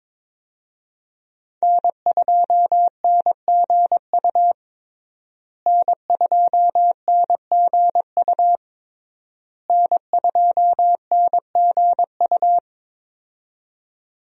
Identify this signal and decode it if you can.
Morse code, heard as a keyed tone, about 21 wpm, 700 Hz. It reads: N2NGU N2NGU N2NGU
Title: USA Call Signs - 5 Characters - Repeated 3x - 22wpm